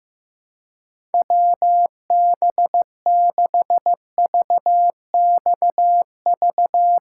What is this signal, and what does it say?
Morse code, heard as a keyed tone, about 15 wpm, 700 Hz. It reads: WB6VXV
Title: USA Callsign DE Callsign 15wpm